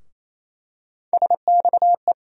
Morse code, heard as a keyed tone, about 28 wpm, 700 Hz. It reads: SXE
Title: USA Call Sign Postfixes - 3 Letters 28wpm